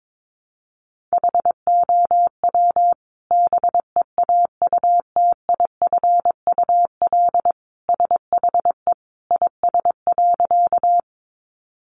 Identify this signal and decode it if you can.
Morse code, heard as a keyed tone, about 22 wpm, 700 Hz. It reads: HOW BEAUTIFUL SHE IS.
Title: Sentences from Top 900 Words 22wpm